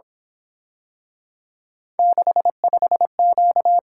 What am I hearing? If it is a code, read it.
Morse code, heard as a keyed tone, about 26 wpm, 700 Hz. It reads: 65Q
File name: Speed-Racing - Three Letter-Number Combinations - 22wpm